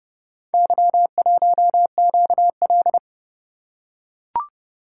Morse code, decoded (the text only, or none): Y1QL